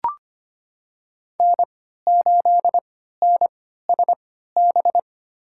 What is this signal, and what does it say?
Morse code, heard as a keyed tone, about 25 wpm, spaced as for 14 wpm, 700 Hz. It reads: N8NSB
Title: USA Call Signs - 5 Characters - 3x Character Spacing - 25wpm